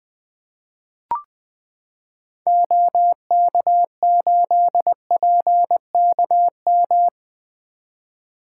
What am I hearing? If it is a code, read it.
Morse code, heard as a keyed tone, about 20 wpm, 700 Hz. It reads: OK8PKM
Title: 2023 SOTA Activator Call Signs - 20wpm